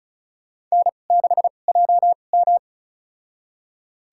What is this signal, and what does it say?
Morse code, heard as a keyed tone, about 34 wpm, spaced as for 25 wpm, 700 Hz. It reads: N6JM
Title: USA Call Signs - 4 Characters - 2x Character Spacing - 35wpm